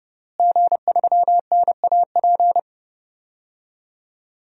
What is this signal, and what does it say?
Morse code, heard as a keyed tone, about 30 wpm, 700 Hz. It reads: G3NAP
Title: DXCC-100 Call Signs - 5 Characters - Repeated 3x - 30wpm